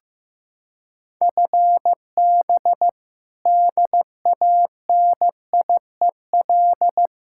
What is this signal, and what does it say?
Morse code, heard as a keyed tone, about 16 wpm, 700 Hz. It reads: FB DANIEL
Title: Practice QSO - Exchange 3 - More Info - 15wpm